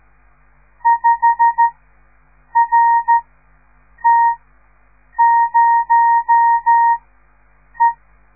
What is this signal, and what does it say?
Morse code, heard as a keyed tone, about 13 wpm, spaced as for 7 wpm, 930 Hz. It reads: 5RT0E
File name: radiomorningstar / morse / les 6